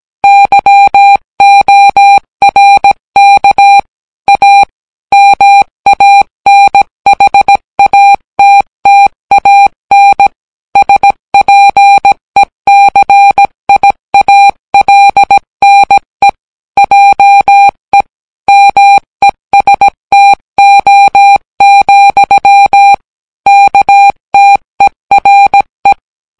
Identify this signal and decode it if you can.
Morse code, heard as a keyed tone, about 17 wpm, 800 Hz. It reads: YORK A MANHATTAN SPECIALNE JE MESTO, KTERE